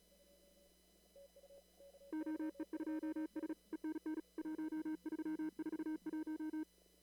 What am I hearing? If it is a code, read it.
Morse code, heard as a keyed tone, about 34 wpm, 310 Hz. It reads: OE2S ENN 1341
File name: CW / kt5j wpx cw-op-k5pi